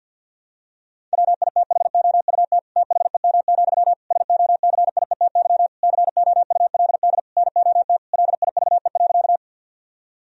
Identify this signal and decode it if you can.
Morse code, heard as a keyed tone, about 50 wpm, 700 Hz. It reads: WITHOUT THEM, SOCIETY COULD NOT LIVE.
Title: Sentences from Top 1000 Words - Review Part 4 - 50wpm